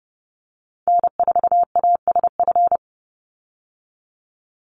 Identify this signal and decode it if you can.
Morse code, heard as a keyed tone, about 30 wpm, 700 Hz. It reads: N4ASF
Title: Call Signs - 30wpm